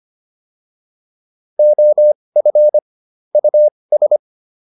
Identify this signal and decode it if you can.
Morse code, heard as a keyed tone, about 25 wpm, spaced as for 20 wpm, 600 Hz. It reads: OF US